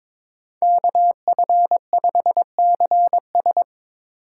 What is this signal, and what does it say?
Morse code, heard as a keyed tone, about 22 wpm, 700 Hz. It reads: KF5CS
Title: USA Call Signs - 8x Character Spacing - 22wpm